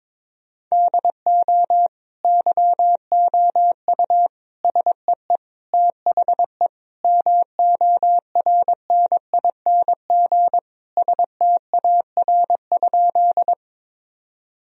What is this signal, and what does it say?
Morse code, heard as a keyed tone, about 22 wpm, 700 Hz. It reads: DO YOU SEE THE MORNING STAR?
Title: Sentences from Top 1000 Words - Review Part 2 - 22wpm